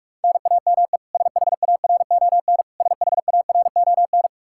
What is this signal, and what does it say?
Morse code, heard as a keyed tone, about 46 wpm, 700 Hz. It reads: NAME SHARON SHARON
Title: Practice QSO - Exchange 1 - Beginning - 45wpm